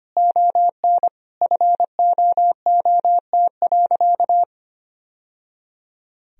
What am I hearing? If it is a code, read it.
Morse code, heard as a keyed tone, about 25 wpm, 700 Hz. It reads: ON FOOT.
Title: Sentences from Top 1000 Words - Review Part 3 - 25wpm